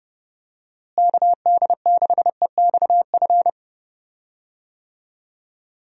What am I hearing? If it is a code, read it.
Morse code, heard as a keyed tone, about 32 wpm, 700 Hz. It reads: KD6EXF
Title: USA Call Signs - 6 Characters - Repeated 3x - 30wpm